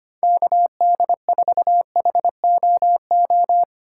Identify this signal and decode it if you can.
Morse code, heard as a keyed tone, about 25 wpm, 700 Hz. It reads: KD4HOO
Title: USA Call Signs - 2x Character Spacing - 25wpm